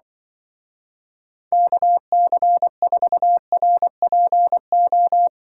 Morse code, heard as a keyed tone, about 24 wpm, 700 Hz. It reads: KC4RPO
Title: Speed Racing - USA 2x3 Call Signs 20wpm